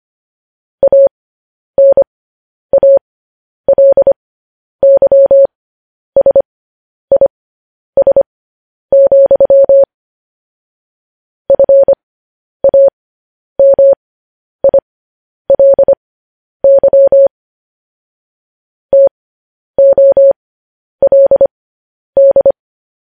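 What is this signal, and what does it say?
Morse code, heard as a keyed tone, about 25 wpm, spaced as for 10 wpm, 550 Hz. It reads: ANALYSIS, FAMILY TOLD
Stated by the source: Morse code news headlines.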